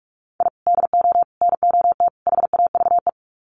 Morse code, heard as a keyed tone, about 44 wpm, 700 Hz. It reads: I DO NOT HAVE
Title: Sets of 4 Words 45wpm